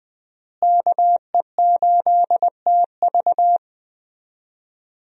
Morse code, heard as a keyed tone, about 20 wpm, 700 Hz. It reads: KE8TV
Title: USA Call Signs - 5 Characters - Repeated 2x - 20wpm